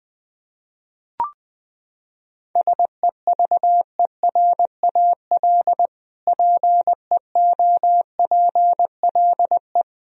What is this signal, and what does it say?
Morse code, heard as a keyed tone, about 20 wpm, 700 Hz. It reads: SEVERAL PEOPLE
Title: Sentences from Top 1000 Words - Review Part 2 - 20wpm